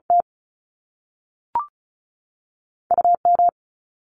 Morse code, heard as a keyed tone, about 36 wpm, 700 Hz. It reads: T U M
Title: Top Two Letter Combinations in Words 35wpm